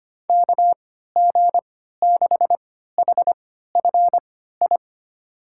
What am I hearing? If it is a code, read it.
Morse code, heard as a keyed tone, about 25 wpm, spaced as for 14 wpm, 700 Hz. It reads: KG6HFI
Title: USA Call Signs - 6 Characters - 3x Character Spacing - 25wpm